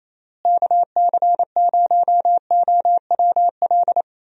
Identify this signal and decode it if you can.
Morse code, heard as a keyed tone, about 28 wpm, 700 Hz. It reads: KC0OWL
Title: USA Call Signs - 6 Characters - Repeated 3x - 28wpm